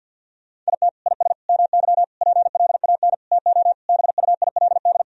Morse code, heard as a keyed tone, about 52 wpm, 700 Hz. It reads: IT IS MY PLAN TO BUILD
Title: Sentences from Top 1000 Words - Review Part 2 - 50wpm